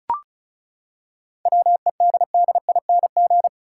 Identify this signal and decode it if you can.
Morse code, heard as a keyed tone, about 34 wpm, 700 Hz. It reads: WEDDING
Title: Seven Letter Words - 35wpm